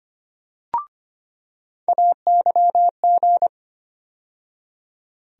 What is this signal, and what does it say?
Morse code, heard as a keyed tone, about 25 wpm, 700 Hz. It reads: AYG
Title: USA Call Sign Postfixes - 3 Letters 25wpm